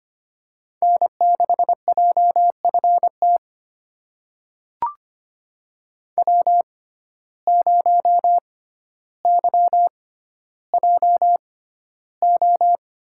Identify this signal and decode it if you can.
Morse code, heard as a keyed tone, about 25 wpm, spaced as for 13 wpm, 700 Hz. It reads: N6JFT W0YJO
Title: USA Call Signs - 5 Characters - 6x Character Spacing - 25wpm